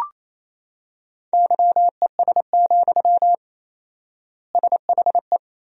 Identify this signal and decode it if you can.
Morse code, heard as a keyed tone, about 27 wpm, 700 Hz. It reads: YES, SHE
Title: 4x Word-Spacing - Sentences from Top 500 Words - 28wpm